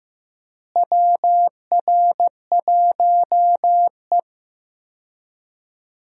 Morse code, heard as a keyed tone, about 15 wpm, 700 Hz. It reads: WR1E
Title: USA Call Signs - 4 Characters - Repeated 3x - 15wpm